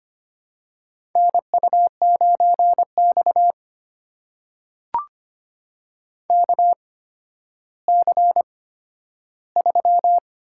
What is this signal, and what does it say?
Morse code, heard as a keyed tone, about 25 wpm, spaced as for 12 wpm, 700 Hz. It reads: NU9X KC3
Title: USA Call Signs - 8x Character Spacing - 25wpm